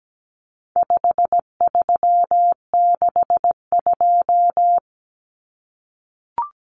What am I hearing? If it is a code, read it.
Morse code, heard as a keyed tone, about 17 wpm, 700 Hz. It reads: 5362
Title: Common Cut Numbers - 4-Digit 17wpm